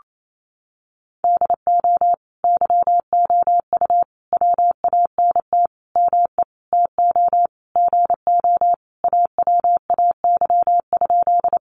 Morse code, heard as a keyed tone, about 28 wpm, 700 Hz. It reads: DO YOU WANT ME TO GO AWAY?
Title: Sentences from Top 600 Words 28wpm